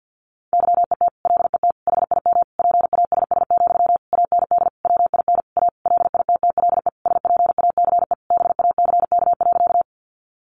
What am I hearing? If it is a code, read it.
Morse code, heard as a keyed tone, about 50 wpm, 700 Hz. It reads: YET LET HIM PASS, AND WIN A LITTLE SPACE BACK.
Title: Sentences from Top 1500 Words 50wpm